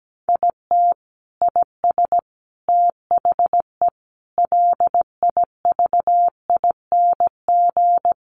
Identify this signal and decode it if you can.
Morse code, heard as a keyed tone, about 17 wpm, 700 Hz. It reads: IT IS THE LIVING
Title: Sentences from Top 1100 Words 17wpm